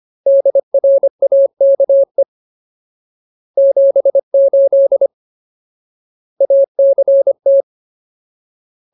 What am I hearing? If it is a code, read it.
Morse code, heard as a keyed tone, about 25 wpm, 550 Hz. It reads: DRAKE 78 ACT
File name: Ultimate Word List / 25WPM Ultimate